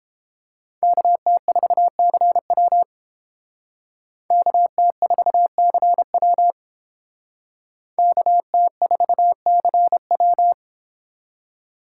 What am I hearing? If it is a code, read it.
Morse code, heard as a keyed tone, about 31 wpm, 700 Hz. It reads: KT4CW KT4CW KT4CW